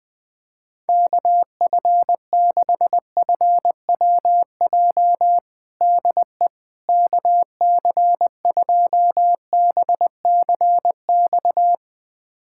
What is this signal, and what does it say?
Morse code, heard as a keyed tone, about 20 wpm, 700 Hz. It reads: KF6FWJ DE KC2BCX
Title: USA Callsign DE Callsign - 6 Characters - 20wpm